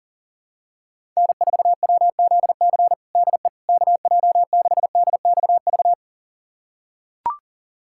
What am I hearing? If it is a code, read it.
Morse code, heard as a keyed tone, about 37 wpm, 700 Hz. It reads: N4WZC DE KJ6DXV
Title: USA Callsign DE Callsign 40wpm